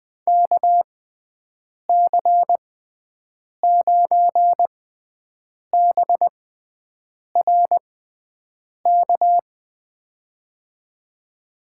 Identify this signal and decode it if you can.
Morse code, heard as a keyed tone, about 21 wpm, spaced as for 7 wpm, 700 Hz. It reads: KC9BRK
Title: USA Call Signs - 6 Characters - 6x Character Spacing - 20wpm